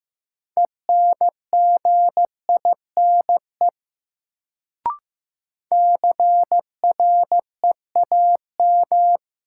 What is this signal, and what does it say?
Morse code, heard as a keyed tone, about 15 wpm, 700 Hz. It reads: ENGINE CREAM
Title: Sentences from Top 1500 Words - New Word Drill - 15wpm